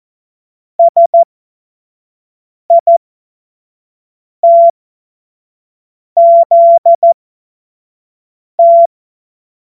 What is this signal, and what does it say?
Morse code, heard as a keyed tone, about 14 wpm, spaced as for 5 wpm, 680 Hz. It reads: SITZT
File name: Morsekurs / Witze / Witz 15. Juni 2021, 25 F70